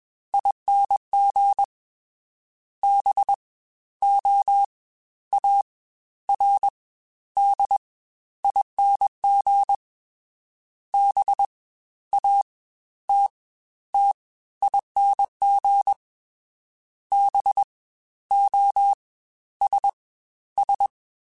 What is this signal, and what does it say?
Morse code, heard as a keyed tone, about 21 wpm, 800 Hz. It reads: ING B O A R D ING B A T T ING B O S S
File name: Special INGs-mp3s / Special INGs-mp3s / Fast ing @20